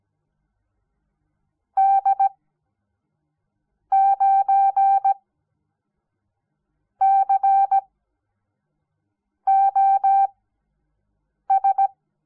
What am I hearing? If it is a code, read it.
Morse code, heard as a keyed tone, about 17 wpm, spaced as for 5 wpm, 780 Hz. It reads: D9COS
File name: Onlinelehrgang / morsen / DJ4UF Morsen06